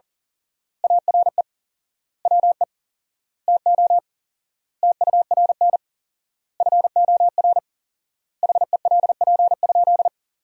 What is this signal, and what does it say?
Morse code, heard as a keyed tone, about 40 wpm, 700 Hz. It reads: ARE WE TO TURN FOR HELP?
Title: 4x Word-Spacing - Sentences from Top 200 Words - 40wpm